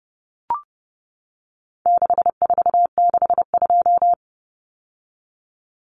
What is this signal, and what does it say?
Morse code, heard as a keyed tone, about 30 wpm, 700 Hz. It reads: 6462
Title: Numbers - 4-Digit 30wpm